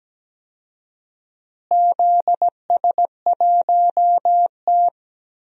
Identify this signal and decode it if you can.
Morse code, heard as a keyed tone, about 17 wpm, 700 Hz. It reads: ZS1T
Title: DXCC-100 Call Signs - 4 Characters - Repeated 3x - 17wpm